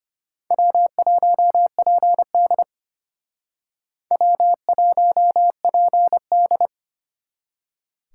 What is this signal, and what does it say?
Morse code, heard as a keyed tone, about 32 wpm, 700 Hz. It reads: W1PD W1PD